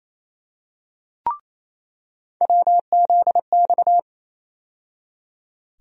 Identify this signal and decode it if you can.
Morse code, heard as a keyed tone, about 27 wpm, 700 Hz. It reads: WZX
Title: USA Call Sign Postfixes - 3 Letters 28wpm